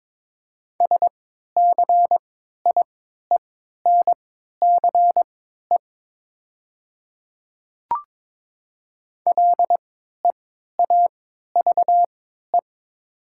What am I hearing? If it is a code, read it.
Morse code, heard as a keyed tone, about 21 wpm, spaced as for 12 wpm, 700 Hz. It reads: SCIENCE LEAVE
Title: 3x Character Spacing - Top 500 Words - 22wpm